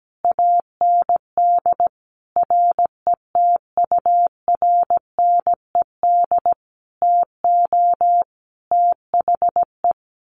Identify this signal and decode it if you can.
Morse code, heard as a keyed tone, about 17 wpm, 700 Hz. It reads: AND RETURNED TO THE
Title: A Sound Of Thunder 17wpm